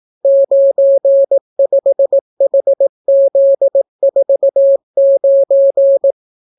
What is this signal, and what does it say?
Morse code, heard as a keyed tone, about 18 wpm, 550 Hz. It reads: 95HZ49